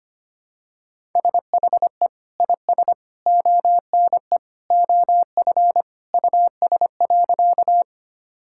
Morse code, heard as a keyed tone, about 25 wpm, 700 Hz. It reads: SHE IS ONE OF US.
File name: practice / 25wpm / Sentences with 3-Word Phrases - 25wpm